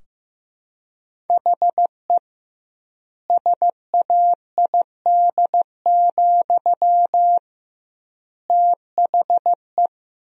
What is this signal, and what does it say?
Morse code, heard as a keyed tone, about 15 wpm, 700 Hz. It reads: HE SAID, THE
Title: 2x Word-Spacing - Sentences from Top 400 Words - 15wpm